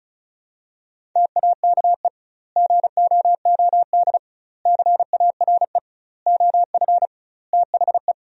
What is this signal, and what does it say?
Morse code, heard as a keyed tone, about 36 wpm, 700 Hz. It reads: TAKE GOOD CARE OF THE
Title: 2x Word-Spacing - Sentences from Top 400 Words - 35wpm